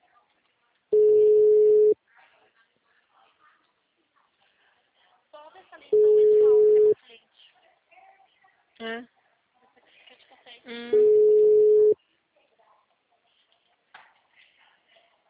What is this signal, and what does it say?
Ringback tone (Europe, 425 Hz), with speech.